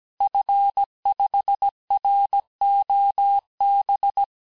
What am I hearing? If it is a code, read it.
Morse code, heard as a keyed tone, about 17 wpm, 800 Hz. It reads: F5ROB
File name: corrige-cours-1581-fin-du-QSO-vitesse-16-mpm